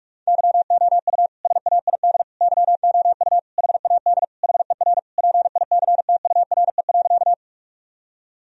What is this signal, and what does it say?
Morse code, heard as a keyed tone, about 46 wpm, 700 Hz. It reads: YOU SAID YOU HAD HER PICTURE.
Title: Sentences from Top 1000 Words - Review Part 1 - 45wpm